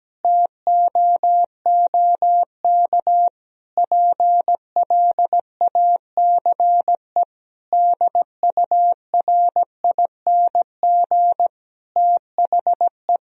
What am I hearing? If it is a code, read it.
Morse code, heard as a keyed tone, about 17 wpm, 700 Hz. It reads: TOOK PLACE DURING THE
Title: Sentences from Top 1000 Words - Review Part 4 - 17wpm